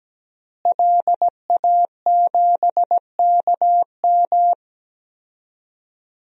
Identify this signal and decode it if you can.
Morse code, heard as a keyed tone, about 17 wpm, 700 Hz. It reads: LA7KM
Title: DXCC-100 Call Signs - 5 Characters - Repeated 2x - 17wpm